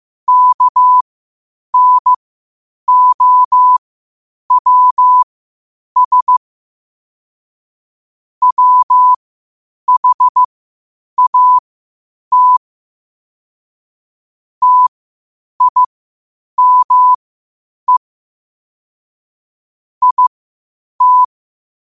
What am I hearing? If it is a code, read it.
Morse code, heard as a keyed tone, about 15 wpm, spaced as for 8 wpm, 1000 Hz. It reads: KNOWS WHAT TIME IT
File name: qotdfiles / 8wpm